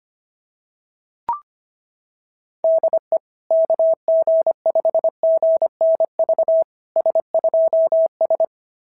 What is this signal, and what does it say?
Morse code, heard as a keyed tone, about 22 wpm, 650 Hz. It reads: DE KG5GNV S2S
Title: SOTA QSO Practice 25wpm